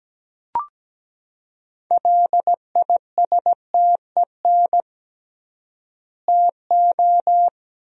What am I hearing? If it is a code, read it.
Morse code, heard as a keyed tone, about 17 wpm, 700 Hz. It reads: LISTEN TO
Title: 3x Word-Spacing - Sentences from Top 400 Words - 17wpm